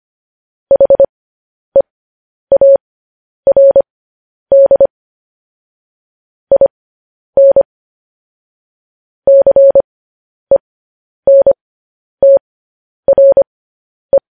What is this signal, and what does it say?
Morse code, heard as a keyed tone, about 25 wpm, spaced as for 10 wpm, 550 Hz. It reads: HEARD IN CENTRE